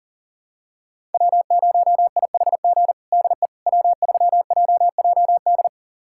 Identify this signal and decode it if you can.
Morse code, heard as a keyed tone, about 37 wpm, 700 Hz. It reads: W0IHG DE W3JJD
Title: USA Callsign DE Callsign - 5 Characters - 40wpm